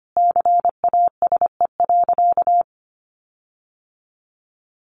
Morse code, heard as a keyed tone, about 25 wpm, 700 Hz. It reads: CASE.
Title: Sentences from Top 1300 Words 25wpm